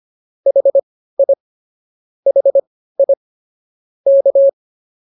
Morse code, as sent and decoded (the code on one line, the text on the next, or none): .... .. / .... .. / -.-
HI HI K